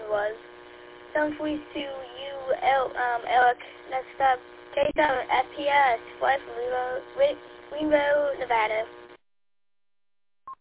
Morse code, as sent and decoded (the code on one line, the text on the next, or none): none